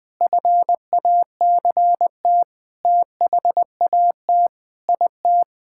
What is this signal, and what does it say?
Morse code, heard as a keyed tone, about 20 wpm, 700 Hz. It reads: FACT THAT IT
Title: Sentences with 8-Word Phrases - 20wpm